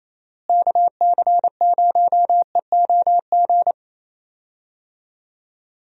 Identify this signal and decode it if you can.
Morse code, heard as a keyed tone, about 28 wpm, 700 Hz. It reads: KC0EOG
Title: USA Call Signs - 6 Characters - Repeated 3x - 28wpm